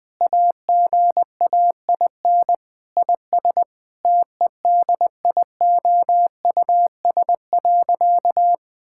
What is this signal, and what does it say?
Morse code, heard as a keyed tone, about 20 wpm, 700 Hz. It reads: AGAIN IS TEDIOUS.
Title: Sentences with 7-Word Phrases - 20wpm